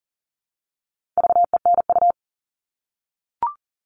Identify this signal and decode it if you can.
Morse code, heard as a keyed tone, about 40 wpm, 700 Hz. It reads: VENU
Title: All Cut Numbers - 4-Digit 40wpm